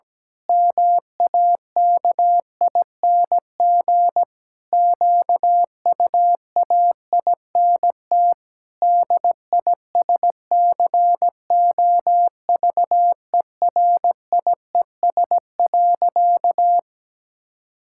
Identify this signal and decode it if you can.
Morse code, heard as a keyed tone, about 17 wpm, 700 Hz. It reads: MAKING QUAINT DISCOVERIES.
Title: Pangram Challenge v2 - 17wpm